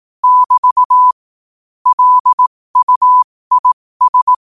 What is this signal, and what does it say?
Morse code, heard as a keyed tone, about 18 wpm, spaced as for 15 wpm, 1000 Hz. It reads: <BT> LUIS
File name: qotdfiles / 15wpm